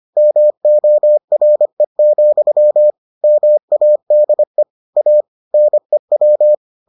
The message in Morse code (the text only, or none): MORE, MADE A NEW